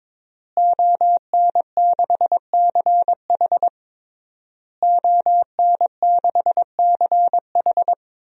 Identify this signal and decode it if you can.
Morse code, heard as a keyed tone, about 22 wpm, 700 Hz. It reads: ON6CH ON6CH
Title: DXCC-100 Call Signs - 5 Characters - Repeated 3x - 22wpm